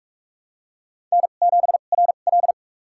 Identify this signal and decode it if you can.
Morse code, heard as a keyed tone, about 45 wpm, spaced as for 32 wpm, 700 Hz. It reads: N7RL
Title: USA Call Signs - 4 Characters - 2x Character Spacing - 45wpm